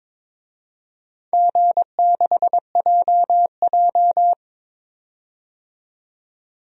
Morse code, heard as a keyed tone, about 22 wpm, 700 Hz. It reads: G6JJ
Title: DXCC-100 Call Signs - 4 Characters - 22wpm